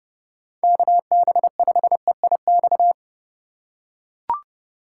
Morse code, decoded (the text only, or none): KB5EIX